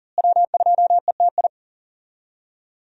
Morse code, heard as a keyed tone, about 40 wpm, 700 Hz. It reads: W2ETI
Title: USA Call Signs - 5 Characters - Repeated 2x - 40wpm